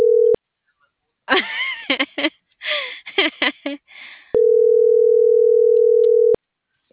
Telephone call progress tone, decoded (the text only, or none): ringback tone